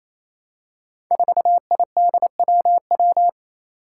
Morse code, heard as a keyed tone, about 28 wpm, 700 Hz. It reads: 4IDWW
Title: Five Challenging Letter-Number Combinations 28wpm